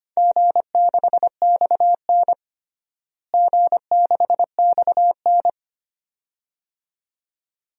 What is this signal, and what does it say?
Morse code, heard as a keyed tone, about 25 wpm, 700 Hz. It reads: G6XN G6XN
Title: DXCC-100 Call Signs - 4 Characters - Repeated 2x - 25wpm